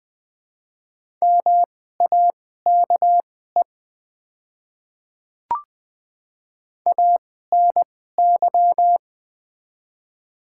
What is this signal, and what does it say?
Morse code, heard as a keyed tone, about 20 wpm, spaced as for 14 wpm, 700 Hz. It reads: MAKE ANY
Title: 2x Character Spacing - Top 200 Words - 20wpm